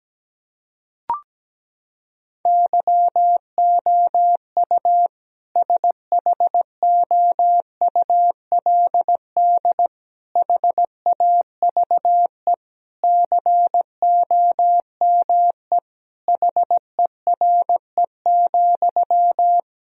Morse code, heard as a keyed tone, about 17 wpm, 700 Hz. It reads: YOU SHOULD HAVE COME HERE,
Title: Sentences from Top 1000 Words - Review Part 4 - 17wpm